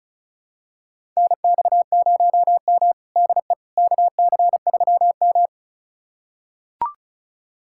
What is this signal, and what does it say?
Morse code, heard as a keyed tone, about 36 wpm, 690 Hz. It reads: NX0M DE KC3M
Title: USA Callsign DE Callsign - 4 Characters - 35wpm